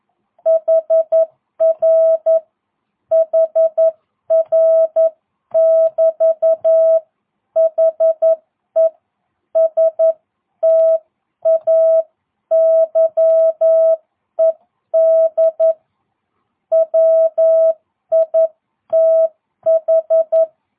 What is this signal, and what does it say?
Morse code, heard as a keyed tone, about 11 wpm, 660 Hz. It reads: HR HR <BT> HE STAYED WITH